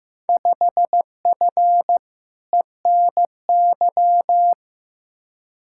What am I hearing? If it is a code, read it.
Morse code, heard as a keyed tone, about 15 wpm, 700 Hz. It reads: 5F ENY
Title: Field Day Exchanges Encore - 15wpm